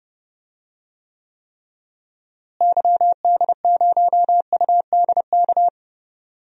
Morse code, heard as a keyed tone, about 32 wpm, 700 Hz. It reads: YD0UDK
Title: DXCC-100 Call Signs - 6 Characters - Repeated 3x - 30wpm